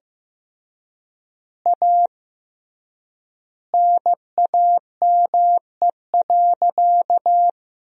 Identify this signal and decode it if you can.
Morse code, heard as a keyed tone, about 15 wpm, 700 Hz. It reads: A NAME.
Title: 3x Word-Spacing - Sentences from Top 500 Words - 15wpm